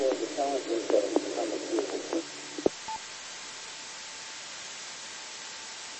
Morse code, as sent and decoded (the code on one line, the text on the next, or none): none